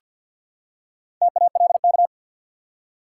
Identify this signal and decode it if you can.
Morse code, heard as a keyed tone, about 49 wpm, 700 Hz. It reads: TALK
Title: Four Letter Words - 50wpm